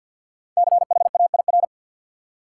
Morse code, heard as a keyed tone, about 53 wpm, 700 Hz. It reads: CHAIR
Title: Five Letter Words - 50wpm